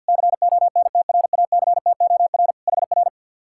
Morse code, heard as a keyed tone, about 66 wpm, spaced as for 58 wpm, 700 Hz. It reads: CONTRACTOR HR
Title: Speed-Racing Common QSO Phrases Encore - 50wpm